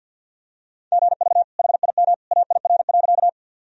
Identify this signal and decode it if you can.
Morse code, heard as a keyed tone, about 50 wpm, 700 Hz. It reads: GV HIM AIR.
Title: Sentences from Top 200 Words and Abbreviations 50wpm